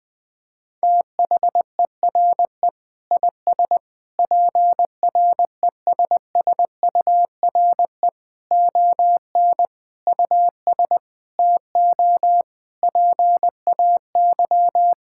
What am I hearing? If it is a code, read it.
Morse code, heard as a keyed tone, about 21 wpm, 700 Hz. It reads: THERE IS PRESSURE ON US TO PAY
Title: Sentences from Top 1000 Words - Review Part 6 - 20wpm